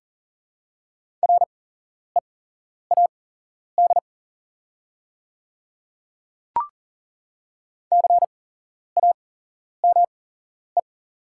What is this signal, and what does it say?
Morse code, heard as a keyed tone, about 40 wpm, spaced as for 11 wpm, 690 Hz. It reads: READ CAME
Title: 8x Character Spacing - Top 200 Words - 40wpm